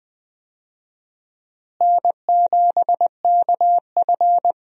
Morse code, heard as a keyed tone, about 20 wpm, 700 Hz. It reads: N7KF
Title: USA Call Signs - 4 Characters - Repeated 3x - 20wpm